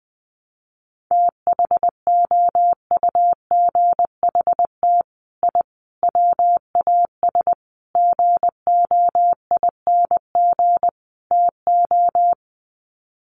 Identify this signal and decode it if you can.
Morse code, heard as a keyed tone, about 20 wpm, 700 Hz. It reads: THOUGHT I WAS GOING TO
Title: Sets of 5 Words Encore2 20wpm